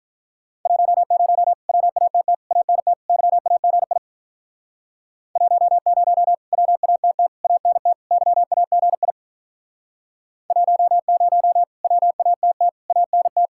Morse code, heard as a keyed tone, about 45 wpm, 700 Hz. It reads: 10 WATT ANT YAGI 10 WATT ANT YAGI 10 WATT ANT